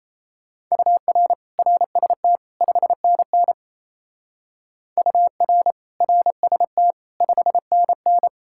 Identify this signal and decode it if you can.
Morse code, heard as a keyed tone, about 36 wpm, 700 Hz. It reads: UR RST 5NN UR RST 5NN